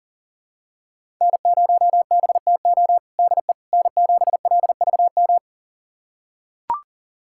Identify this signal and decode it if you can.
Morse code, heard as a keyed tone, about 37 wpm, 700 Hz. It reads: N0BTO DE N7LVM
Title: USA Callsign DE Callsign - 5 Characters - 40wpm